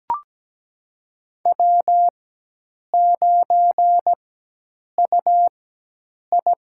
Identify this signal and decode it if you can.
Morse code, heard as a keyed tone, about 17 wpm, spaced as for 8 wpm, 700 Hz. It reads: W9UI
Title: USA Call Signs - 4 Characters - 4x Character Spacing - 17wpm